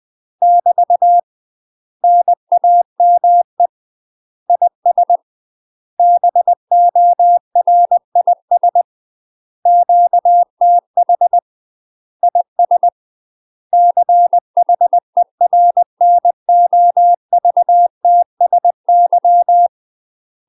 Morse code, heard as a keyed tone, about 20 wpm, 700 Hz. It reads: <BT> NAME IS BORIS QTH IS CHERNOVTSY
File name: UK MASTER QSOs TRAINING / 20WPM / UK MASTER QSOs 0068